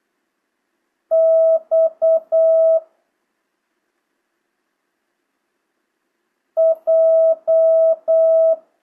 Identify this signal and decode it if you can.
Morse code, heard as a keyed tone, about 8 wpm, 650 Hz. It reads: X J
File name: jh8cbh / 2025cw